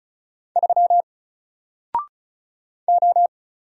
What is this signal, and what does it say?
Morse code, heard as a keyed tone, about 34 wpm, 700 Hz. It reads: 3 O
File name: Single-Letter-Number-35wpm